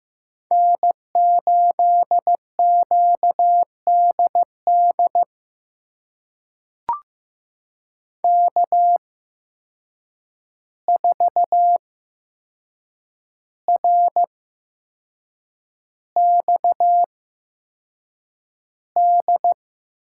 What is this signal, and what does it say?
Morse code, heard as a keyed tone, about 15 wpm, spaced as for 6 wpm, 700 Hz. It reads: N8QDD K4RXD
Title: USA Call Signs - 5 Characters - 8x Character Spacing - 15wpm